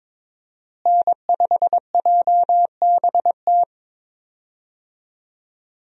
Morse code, heard as a keyed tone, about 22 wpm, 700 Hz. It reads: N5JBT
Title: USA Call Signs - 5 Characters - Repeated 3x - 22wpm